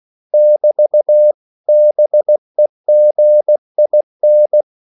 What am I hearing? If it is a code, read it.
Morse code, heard as a keyed tone, about 16 wpm, 600 Hz. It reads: <BT> BEGIN
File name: CW / Practice / 500MostCommonEnglishWords16